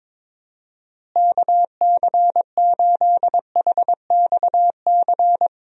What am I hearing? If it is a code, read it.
Morse code, heard as a keyed tone, about 22 wpm, 700 Hz. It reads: KC8HXC
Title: USA Call Signs - 6 Characters - Repeated 2x - 22wpm